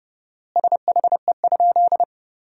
Morse code, heard as a keyed tone, about 30 wpm, 700 Hz. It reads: SHE?
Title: 4x Word-Spacing - Sentences from Top 100 Words - 30wpm